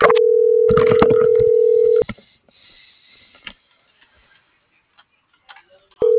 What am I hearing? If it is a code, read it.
Ringback tone (North America), 440 + 480 Hz.